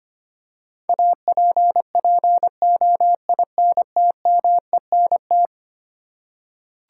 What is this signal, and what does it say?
Morse code, heard as a keyed tone, about 25 wpm, 700 Hz. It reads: APPOINTMENT
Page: morse-code-ninja-practice-25wpm